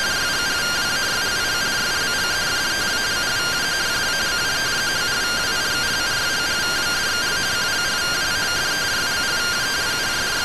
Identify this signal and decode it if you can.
Two-tone teleprinter signal, about 170 Hz shift.